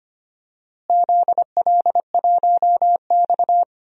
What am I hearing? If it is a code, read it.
Morse code, heard as a keyed tone, about 25 wpm, 700 Hz. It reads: ZL1X
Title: DXCC-100 Call Signs - 4 Characters - Repeated 3x - 25wpm